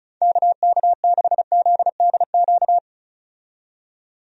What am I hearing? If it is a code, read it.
Morse code, heard as a keyed tone, about 36 wpm, 700 Hz. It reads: KK6ZDQ
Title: USA Call Signs - 6 Characters - 35wpm